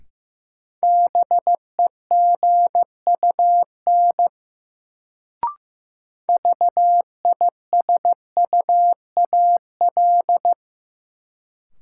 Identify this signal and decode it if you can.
Morse code, heard as a keyed tone, about 15 wpm, 700 Hz. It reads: BEGUN VISUAL
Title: Sentences from Top 1000 Words - Review Part 6 - New Word Drill - 15wpm